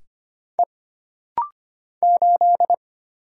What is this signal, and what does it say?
Morse code, heard as a keyed tone, about 25 wpm, 700 Hz. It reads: E 8
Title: Single Letter-Number Alt - Rapid-Fire 25wpm